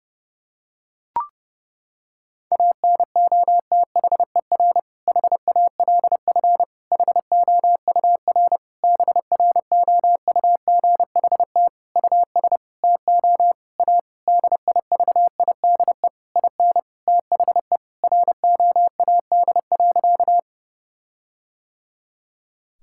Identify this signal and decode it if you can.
Morse code, heard as a keyed tone, about 30 wpm, 700 Hz. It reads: ANOTHER HALF HOUR BROUGHT US TO A DIVIDE IN THE ROAD.
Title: Sentences from Top 1600 Words 30wpm